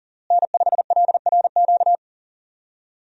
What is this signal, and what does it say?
Morse code, heard as a keyed tone, about 40 wpm, 700 Hz. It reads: N5LRQ
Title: USA Call Signs - 5 Characters - Repeated 2x - 40wpm